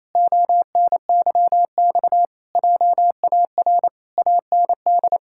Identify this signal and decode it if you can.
Morse code, heard as a keyed tone, about 28 wpm, 700 Hz. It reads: ONYX JAR AND
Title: Pangram Challenge v2 - 28wpm